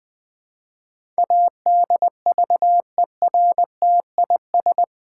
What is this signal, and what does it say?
Morse code, heard as a keyed tone, about 20 wpm, 700 Hz. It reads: ADVERTIS
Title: Build A Word - 4 Letters At A Time - 11-Letter Words - 20wpm